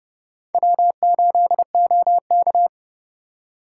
Morse code, heard as a keyed tone, about 36 wpm, 700 Hz. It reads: W8OK
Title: Speed Racing - USA Call Signs 30wpm